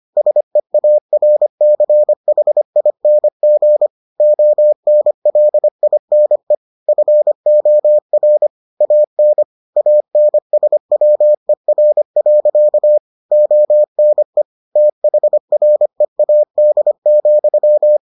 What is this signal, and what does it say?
Morse code, heard as a keyed tone, about 25 wpm, 600 Hz. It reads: SEARCHING ONLINE FOR AN ANSWER. ONE THREAD,